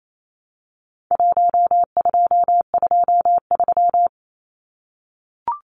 Morse code, heard as a keyed tone, about 28 wpm, 700 Hz. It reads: 1223
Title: Numbers - Contest Style - 4-Digit 28wpm